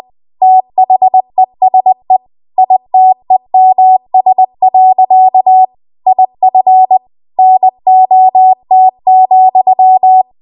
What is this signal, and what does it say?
Morse code, heard as a keyed tone, about 20 wpm, 750 Hz. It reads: THESE ITEMS. IF NOT,